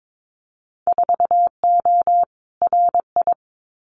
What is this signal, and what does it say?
Morse code, heard as a keyed tone, about 22 wpm, 700 Hz. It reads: 4O RI
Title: Winter Field Day Exchanges Encore 22wpm